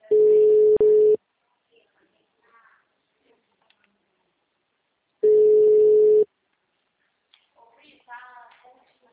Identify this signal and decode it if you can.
Ringback tone (Europe, 425 Hz), with speech.